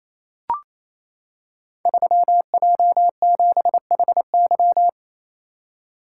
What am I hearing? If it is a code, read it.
Morse code, heard as a keyed tone, about 27 wpm, 700 Hz. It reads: 3J7HY
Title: Code Groups - 5 Letter:Numbers - 28wpm